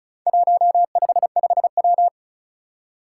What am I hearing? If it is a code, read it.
Morse code, heard as a keyed tone, about 36 wpm, 700 Hz. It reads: 155W
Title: Four Challenging Letter-Number Combinations 35wpm